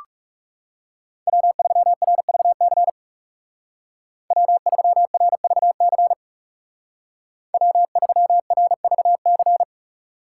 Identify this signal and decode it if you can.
Morse code, heard as a keyed tone, about 39 wpm, 700 Hz. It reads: W3RVC W3RVC W3RVC